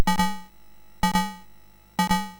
There is no code.